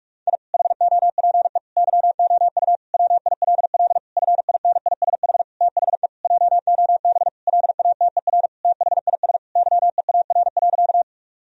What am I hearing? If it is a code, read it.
Morse code, heard as a keyed tone, about 42 wpm, 700 Hz. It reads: I HOPE YOU WILL FINISH THE JOB LATER THIS YEAR.
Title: Sentences from Top 1000 Words - Review Part 3 - 45wpm